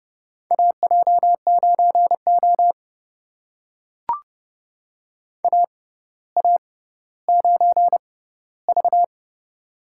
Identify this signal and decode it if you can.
Morse code, heard as a keyed tone, about 30 wpm, spaced as for 15 wpm, 700 Hz. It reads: AJ9O AA9V
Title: USA Call Signs - 4 Characters - 6x Character Spacing - 30wpm